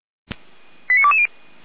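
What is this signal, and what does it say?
Five-tone selective call (ZVEI), heard as quick runs of five tones, tones of about 70 ms. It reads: 89220
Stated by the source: ZVEI five-tone selective call received by SDR.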